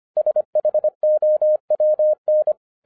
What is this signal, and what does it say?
Morse code, heard as a keyed tone, about 25 wpm, 610 Hz. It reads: SHOWN